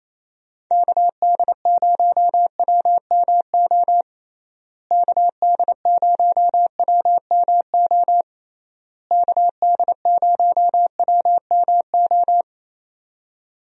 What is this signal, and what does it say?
Morse code, heard as a keyed tone, about 27 wpm, 700 Hz. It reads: KD0WMO KD0WMO KD0WMO
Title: USA Call Signs - 6 Characters - Repeated 3x - 28wpm